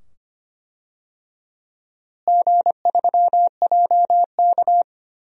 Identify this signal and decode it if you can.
Morse code, heard as a keyed tone, about 25 wpm, 700 Hz. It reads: G3JK
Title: DXCC-100 Call Signs - 4 Characters - Repeated 2x - 25wpm